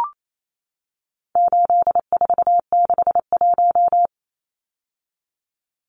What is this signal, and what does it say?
Morse code, heard as a keyed tone, about 28 wpm, 700 Hz. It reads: 8461
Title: Numbers - 4-Digit 28wpm